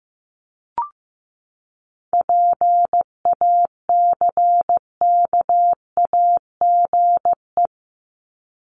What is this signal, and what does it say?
Morse code, heard as a keyed tone, about 15 wpm, 700 Hz. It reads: PACKAGE
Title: Top 1501-1750 Words 15wpm